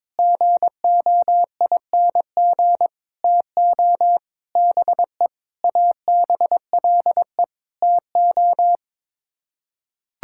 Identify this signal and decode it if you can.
Morse code, heard as a keyed tone, about 22 wpm, 700 Hz. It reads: GOING TO BE ABLE TO
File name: practice / 22wpm / Sets of 7 Words - 22wpm